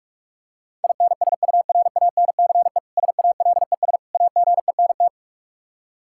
Morse code, heard as a keyed tone, about 46 wpm, 700 Hz. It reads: INSURANCE SALES AGENT
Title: Practice QSO - Exchange 3 - More Info - 45wpm